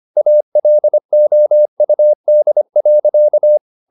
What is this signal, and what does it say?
Morse code, heard as a keyed tone, about 25 wpm, 600 Hz. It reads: ALOUD.
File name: the old man and the sea part 1 25wpm